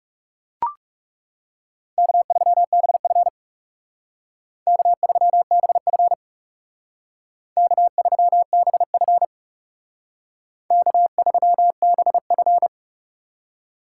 Morse code, heard as a keyed tone, about 34 wpm, 700 Hz. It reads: K3BF K3BF K3BF K3BF
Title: Speed Racing - USA 1x2 Call Signs 30wpm